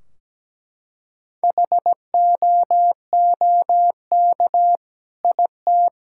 Morse code, heard as a keyed tone, about 17 wpm, 700 Hz. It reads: HOOK IT
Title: Sentences from Top 1600 Words 17wpm